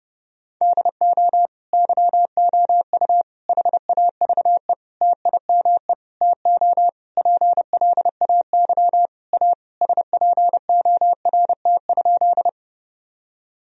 Morse code, heard as a keyed tone, about 30 wpm, 700 Hz. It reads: DO YOU HAVE TIME TO PLAY A SPORT?
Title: Sentences from Top 1000 Words - Review Part 4 - 30wpm